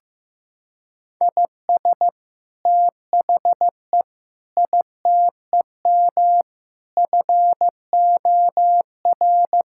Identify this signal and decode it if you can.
Morse code, heard as a keyed tone, about 14 wpm, 700 Hz. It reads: IS THE ITEM FOR
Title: Sentences from Top 1000 Words - Review Part 6 - 15wpm